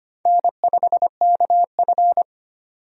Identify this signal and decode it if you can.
Morse code, heard as a keyed tone, about 25 wpm, 700 Hz. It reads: N5KF
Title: USA Call Signs - 4 Characters - 25wpm